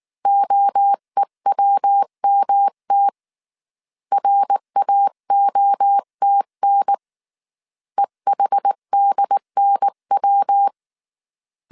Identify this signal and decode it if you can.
Morse code, heard as a keyed tone, about 19 wpm, 800 Hz. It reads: OEWMT RAOTN EHDNW
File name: CW / DET / 1 3 1500C S B 18wpm 1916